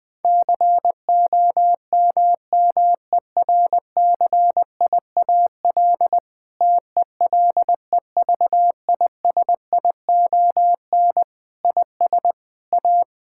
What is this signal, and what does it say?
Morse code, heard as a keyed tone, about 20 wpm, 700 Hz. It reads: COMMERCIAL TELEVISION IS A
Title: Sentences from Top 1000 Words - Review Part 6 - 20wpm